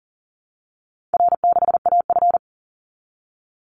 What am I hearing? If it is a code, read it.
Morse code, heard as a keyed tone, about 40 wpm, 700 Hz. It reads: R6AF
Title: Easy Call Signs Repeated 3x - 40wpm